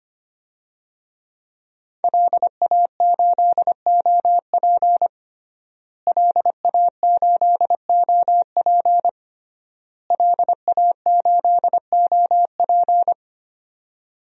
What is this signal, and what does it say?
Morse code, heard as a keyed tone, about 25 wpm, 700 Hz. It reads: LA8OP LA8OP LA8OP
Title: DXCC-100 Call Signs - 5 Characters - Repeated 3x - 25wpm